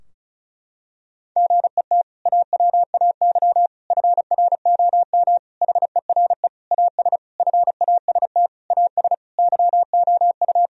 Morse code, heard as a keyed tone, about 34 wpm, 700 Hz. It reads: GET AWAY FROM HERE AS FAST AS YOU
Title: Sentences from Top 1000 Words - Review Part 2 - 35wpm